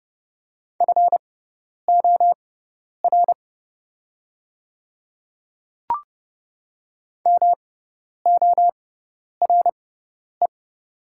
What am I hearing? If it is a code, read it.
Morse code, heard as a keyed tone, about 30 wpm, spaced as for 10 wpm, 700 Hz. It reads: FOR MORE
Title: 6x Character Spacing - Top 300 Words - 30wpm